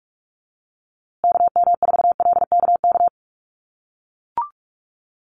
Morse code, heard as a keyed tone, about 44 wpm, 700 Hz. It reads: KM4LKK
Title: USA Call Signs Repeated 2x 45wpm